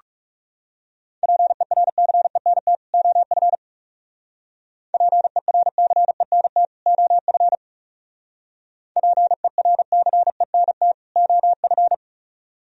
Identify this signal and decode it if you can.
Morse code, heard as a keyed tone, about 36 wpm, 700 Hz. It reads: PERCENT OF PERCENT OF PERCENT OF
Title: Speed-Racing - Sets of 2 Words 30wpm